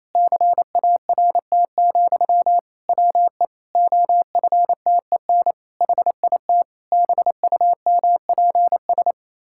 Morse code, heard as a keyed tone, about 28 wpm, 700 Hz. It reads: CART, WE OFTEN HIT BUMPS
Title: Pangram Challenge v2 - 28wpm